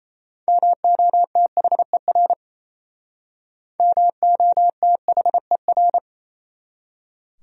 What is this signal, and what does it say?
Morse code, heard as a keyed tone, about 32 wpm, 700 Hz. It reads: MOTHER MOTHER
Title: Speed-Racing Top 300 Words 28wpm